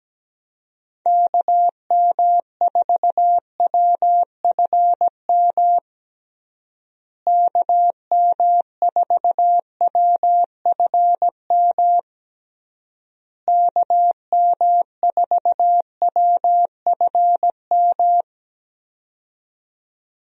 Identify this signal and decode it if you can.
Morse code, heard as a keyed tone, about 17 wpm, 700 Hz. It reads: KM4WFM KM4WFM KM4WFM
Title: USA Call Signs - 6 Characters - Repeated 3x - 17wpm